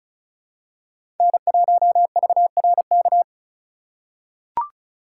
Morse code, heard as a keyed tone, about 36 wpm, 700 Hz. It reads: N1VRK
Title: USA Call Signs - 3x Character Spacing - 35wpm